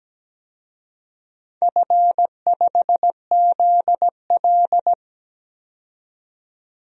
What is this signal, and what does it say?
Morse code, heard as a keyed tone, about 17 wpm, 700 Hz. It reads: F5ZL
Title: DXCC-100 Call Signs - 4 Characters - 17wpm